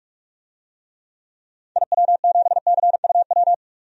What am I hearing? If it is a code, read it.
Morse code, heard as a keyed tone, about 41 wpm, 700 Hz. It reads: IW7CUW
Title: DXCC-100 Call Signs - 6 Characters - 45wpm